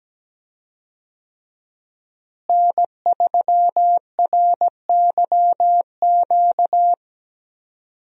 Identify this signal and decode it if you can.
Morse code, heard as a keyed tone, about 17 wpm, 700 Hz. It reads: N3RYQ
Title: USA Call Signs - 5 Characters - Repeated 2x - 17wpm